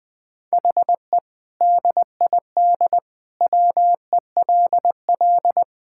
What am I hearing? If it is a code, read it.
Morse code, heard as a keyed tone, about 20 wpm, 700 Hz. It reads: HE DID WELL
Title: Sentences with 3-Word Phrases - Spoken Beginning - 20wpm